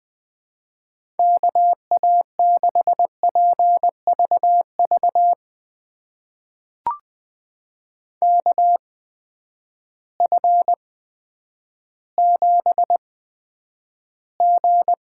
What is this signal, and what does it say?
Morse code, heard as a keyed tone, about 20 wpm, spaced as for 10 wpm, 700 Hz. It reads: KA6PVV KF7G
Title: USA Call Signs - 6 Characters - 8x Character Spacing - 20wpm